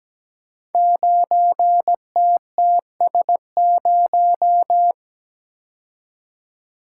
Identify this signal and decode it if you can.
Morse code, heard as a keyed tone, about 17 wpm, 700 Hz. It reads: 9TTS0
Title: Code Groups - 5 Letter:Numbers - 17wpm